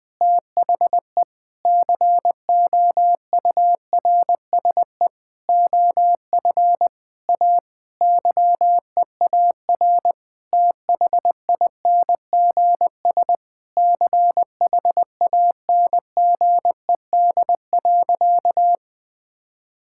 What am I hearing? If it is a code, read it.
Morse code, heard as a keyed tone, about 20 wpm, 700 Hz. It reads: THE COURSE OF A YEAR THINGS CHANGED.
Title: Sentences with 4-Word Phrases - 20wpm